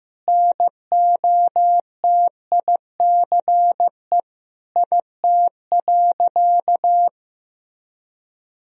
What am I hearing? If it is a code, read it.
Morse code, heard as a keyed tone, about 15 wpm, 700 Hz. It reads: NOTICE IT.
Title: Sentences from Top 1000 Words - Review Part 2 - 15wpm